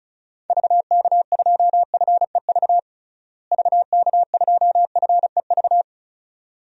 Morse code, heard as a keyed tone, about 36 wpm, 700 Hz. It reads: VK2FEV VK2FEV
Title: DXCC-100 Call Signs - 6 Characters - Repeated 2x - 35wpm